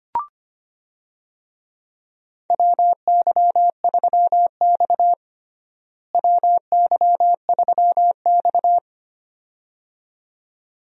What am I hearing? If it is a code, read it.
Morse code, heard as a keyed tone, about 25 wpm, 700 Hz. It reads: WY3X WY3X
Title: USA Call Signs - 4 Characters - Repeated 2x - 25wpm